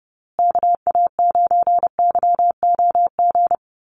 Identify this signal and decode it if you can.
Morse code, heard as a keyed tone, about 30 wpm, 700 Hz. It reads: KA9YOG
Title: USA Call Signs Repeated 2x 30wpm